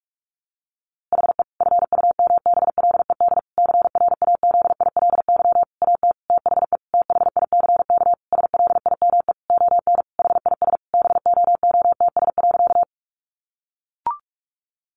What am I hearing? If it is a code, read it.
Morse code, heard as a keyed tone, about 44 wpm, 700 Hz. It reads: HE FUMBLED CRAZILY AT THE THICK SLIME ON HIS BOOTS.
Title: A Sound Of Thunder 45wpm